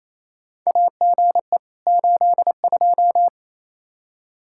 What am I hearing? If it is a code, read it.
Morse code, heard as a keyed tone, about 28 wpm, 700 Hz. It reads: AGE 82
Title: Practice QSO - Exchange 3 - More Info - 28wpm